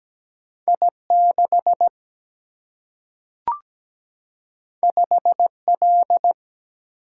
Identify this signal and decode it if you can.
Morse code, heard as a keyed tone, about 17 wpm, 700 Hz. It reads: I 6 5 L
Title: Single-Letter Single-Number 17wpm